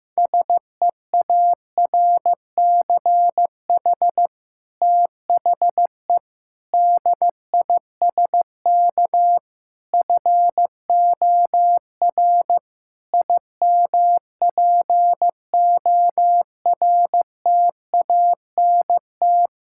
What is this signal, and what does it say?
Morse code, heard as a keyed tone, about 14 wpm, 700 Hz. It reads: SEARCH THE DISK FOR IMPORTANT
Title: Sentences from Top 1000 Words - Review Part 6 - 15wpm